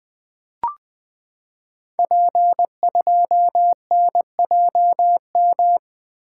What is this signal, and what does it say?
Morse code, heard as a keyed tone, about 20 wpm, 700 Hz. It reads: P2NJM
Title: Code Groups - 5 Letter:Numbers - 20wpm